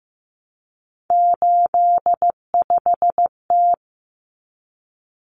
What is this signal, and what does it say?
Morse code, heard as a keyed tone, about 15 wpm, 700 Hz. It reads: 85T
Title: Common Cut Numbers - 3-Digit 15wpm